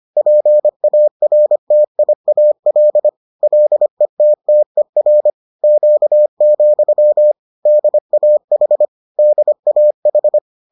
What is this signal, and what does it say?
Morse code, heard as a keyed tone, about 25 wpm, 600 Hz. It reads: PARTIAL LETTER Q, DAH DAH